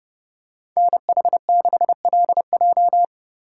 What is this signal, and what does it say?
Morse code, heard as a keyed tone, about 30 wpm, 700 Hz. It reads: NH6LJ